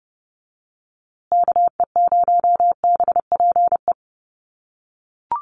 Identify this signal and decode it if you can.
Morse code, heard as a keyed tone, about 30 wpm, 700 Hz. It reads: KE0BPE
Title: USA Call Signs Repeated 3x 30wpm